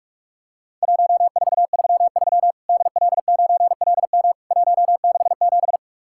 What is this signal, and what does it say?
Morse code, heard as a keyed tone, about 44 wpm, 700 Hz. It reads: 1433 DL9LM 167
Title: Worked All Europe DX Contest - QTC Exchange - 45wpm